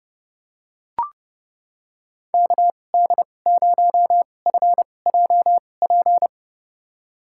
Morse code, heard as a keyed tone, about 30 wpm, spaced as for 22 wpm, 700 Hz. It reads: KD0FJP
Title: USA Call Signs - 6 Characters - 2x Character Spacing - 30wpm